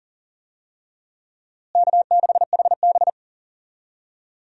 Morse code, heard as a keyed tone, about 37 wpm, 700 Hz. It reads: K6HB
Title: USA Call Signs - 4 Characters - Repeated 2x - 40wpm